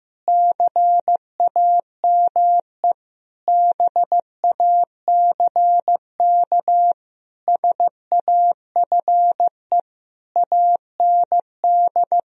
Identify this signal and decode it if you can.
Morse code, heard as a keyed tone, about 15 wpm, 700 Hz. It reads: CAME BACK SAFE AND
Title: Sentences from Top 1000 Words - Review Part 1 - 15wpm